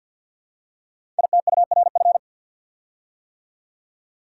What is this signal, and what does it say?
Morse code, heard as a keyed tone, about 53 wpm, 700 Hz. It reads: ITURF
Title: Code Groups - 5 Letters - 50wpm